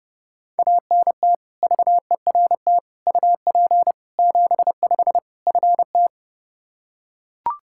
Morse code, heard as a keyed tone, about 30 wpm, 700 Hz. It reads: ANT VERT UP 75 FT
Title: K1USN SST QSO Practice 30wpm